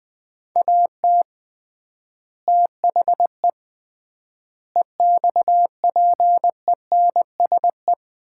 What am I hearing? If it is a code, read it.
Morse code, heard as a keyed tone, about 21 wpm, 700 Hz. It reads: AT THE EXPENSE
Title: 3x Word-Spacing - Sets of 4 Words - 20wpm